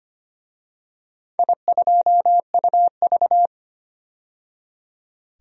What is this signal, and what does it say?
Morse code, heard as a keyed tone, about 25 wpm, 700 Hz. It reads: I2UV
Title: DXCC-100 Call Signs - 4 Characters - 25wpm